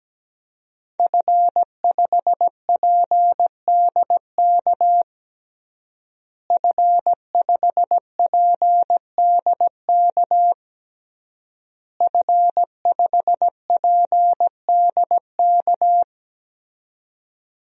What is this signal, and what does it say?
Morse code, heard as a keyed tone, about 17 wpm, 700 Hz. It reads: F5PDK F5PDK F5PDK
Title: DXCC-100 Call Signs - 5 Characters - Repeated 3x - 17wpm